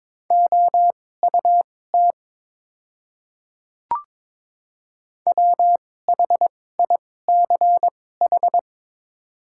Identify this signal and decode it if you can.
Morse code, heard as a keyed tone, about 22 wpm, spaced as for 16 wpm, 700 Hz. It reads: OUT WHICH
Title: 2x Character Spacing - Top 500 Words - 22wpm